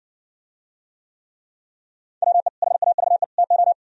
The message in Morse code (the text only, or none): WE HAVE TO